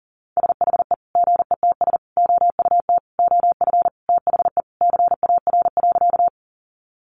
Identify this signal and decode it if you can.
Morse code, heard as a keyed tone, about 40 wpm, 700 Hz. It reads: SHE GETS OUT OF THE CAR.
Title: Easy Phrases 40wpm